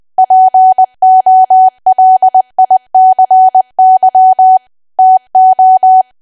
Morse code, heard as a keyed tone, about 20 wpm, 750 Hz. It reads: POLICY TO
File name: cwa / W1AW-Level3 / 20WPM